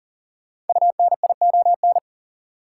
Morse code, heard as a keyed tone, about 40 wpm, 700 Hz. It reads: UNION
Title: Sentences from Top 1400 Words - New Word Drill Alt - 40wpm